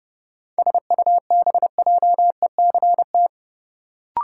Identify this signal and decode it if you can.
Morse code, heard as a keyed tone, about 30 wpm, 700 Hz. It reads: SUBJECT
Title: Sentences from Top 800 Words - New Word Drill - 30wpm